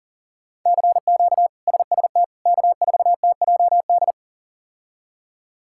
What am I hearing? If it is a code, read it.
Morse code, heard as a keyed tone, about 40 wpm, 700 Hz. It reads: CQ SST K4TJD
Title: K1USN SST QSO Practice - 40wpm